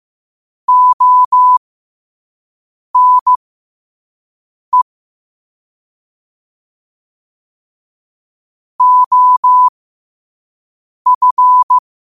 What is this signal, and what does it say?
Morse code, heard as a keyed tone, about 15 wpm, spaced as for 5 wpm, 1000 Hz. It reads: ONE OF
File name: qotdfiles / 5wpm